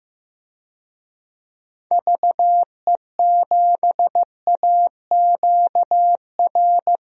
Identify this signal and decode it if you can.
Morse code, heard as a keyed tone, about 15 wpm, 700 Hz. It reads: VE7AQR
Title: DXCC-100 Call Signs - 6 Characters - Repeated 2x - 15wpm